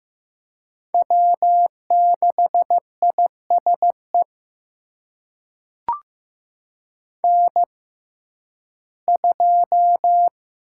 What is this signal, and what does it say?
Morse code, heard as a keyed tone, about 15 wpm, spaced as for 11 wpm, 700 Hz. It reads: W6ISE N2
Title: USA Call Signs - 5 Characters - 6x Character Spacing - 15wpm